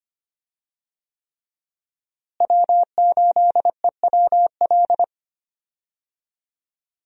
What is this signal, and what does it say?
Morse code, heard as a keyed tone, about 25 wpm, 700 Hz. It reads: W8EWL
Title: USA Call Signs - 5 Characters - Repeated 3x - 25wpm